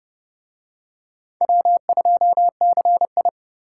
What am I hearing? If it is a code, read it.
Morse code, heard as a keyed tone, about 29 wpm, 700 Hz. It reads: W2CI